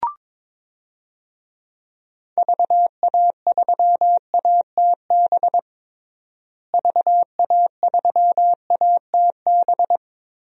Morse code, heard as a keyed tone, about 22 wpm, 700 Hz. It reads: VA3ATB VA3ATB
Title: DXCC-100 Call Signs - 6 Characters - Repeated 3x - 22wpm